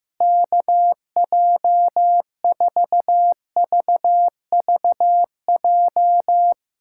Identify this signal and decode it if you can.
Morse code, heard as a keyed tone, about 15 wpm, 700 Hz. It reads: KJ4VVJ
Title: USA Callsign DE Callsign 15wpm